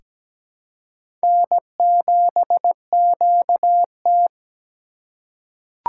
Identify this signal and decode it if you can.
Morse code, heard as a keyed tone, about 17 wpm, 700 Hz. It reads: N7QT
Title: K1USN SST QSO Practice - 17wpm